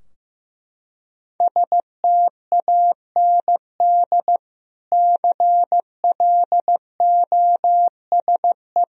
Morse code, heard as a keyed tone, about 16 wpm, 700 Hz. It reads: STAND CLOSE
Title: Sentences from Top 1000 Words - Review Part 2 - 15wpm